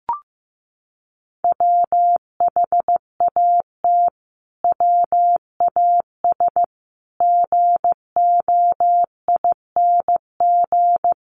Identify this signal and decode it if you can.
Morse code, heard as a keyed tone, about 15 wpm, 700 Hz. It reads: WHAT WAS GOING
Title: Sets of 5 Words Encore 15wpm